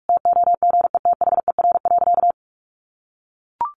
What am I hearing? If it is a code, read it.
Morse code, heard as a keyed tone, about 44 wpm, 700 Hz. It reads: TOGETHER.
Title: Sentences from Top 1500 Words 45wpm